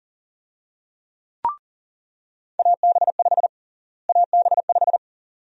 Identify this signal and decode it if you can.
Morse code, heard as a keyed tone, about 40 wpm, 700 Hz. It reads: AB5 AB5
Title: USA Call Sign Prefixes - Repeated 2x - 40wpm